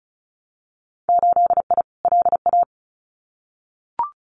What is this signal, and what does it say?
Morse code, heard as a keyed tone, about 36 wpm, 700 Hz. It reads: 8I LA
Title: Winter Field Day Exchanges Encore 35wpm